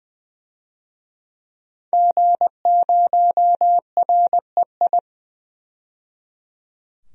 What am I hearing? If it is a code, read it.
Morse code, heard as a keyed tone, about 20 wpm, 700 Hz. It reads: G0REI
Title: DXCC-100 Call Signs - 5 Characters - 20wpm